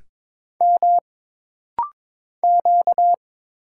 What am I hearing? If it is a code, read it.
Morse code, heard as a keyed tone, about 22 wpm, 700 Hz. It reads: M Q
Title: Single Letter-Number Alt - Rapid-Fire 22wpm